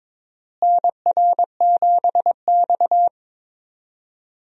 Morse code, heard as a keyed tone, about 22 wpm, 700 Hz. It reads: NR7X
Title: USA Call Signs - 4 Characters - Repeated 3x - 22wpm